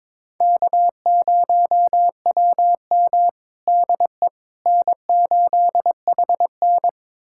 Morse code, heard as a keyed tone, about 22 wpm, 700 Hz. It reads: K0WM DE N8HN
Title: USA Callsign DE Callsign - 4 Characters - 22wpm